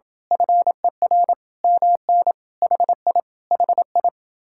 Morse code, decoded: FER MN HI HI

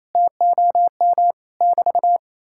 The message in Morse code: TOM <BT>

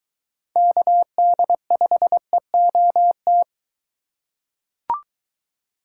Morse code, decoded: KD5EOT

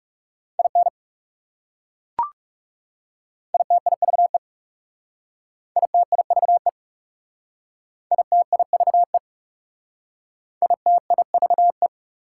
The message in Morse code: IN ITIVE ITIVE ITIVE ITIVE

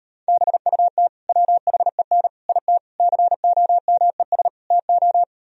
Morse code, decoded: BUT WHEN IT COMES TO

